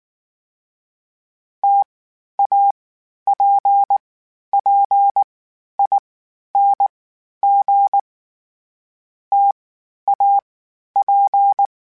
Morse code, heard as a keyed tone, about 19 wpm, 800 Hz. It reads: T A P P I N G T A P
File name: Difficult Combos / Difficult Combos-mp3s / T AOEI @18